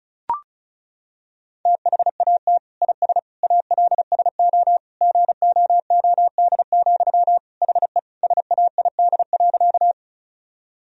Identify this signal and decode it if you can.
Morse code, heard as a keyed tone, about 36 wpm, 700 Hz. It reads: THAT IS ALSO GOOD, HE SAID.
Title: Sentences from Top 1000 Words - Review Part 1 - 35wpm